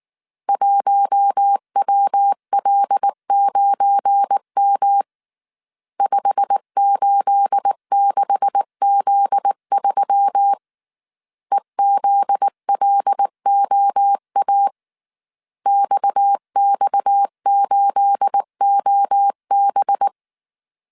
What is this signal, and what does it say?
Morse code, heard as a keyed tone, about 19 wpm, 800 Hz. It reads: 1WL9M 586Z3 EZLOA XX8OB
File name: CW / DET / 1 9 1500C S A 18wpm 2459